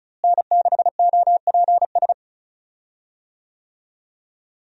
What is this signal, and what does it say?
Morse code, heard as a keyed tone, about 34 wpm, 700 Hz. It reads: N6OPS
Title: USA Call Signs - 5 Characters - Repeated 2x - 35wpm